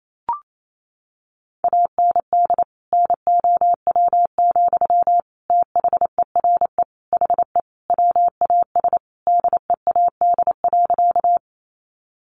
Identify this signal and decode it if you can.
Morse code, heard as a keyed tone, about 28 wpm, 700 Hz. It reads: AND NOW, THERE HE WAS DEAD.
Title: Sentences from Top 1500 Words 28wpm